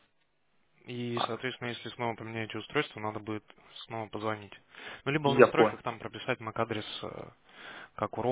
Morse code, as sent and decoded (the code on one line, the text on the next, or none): none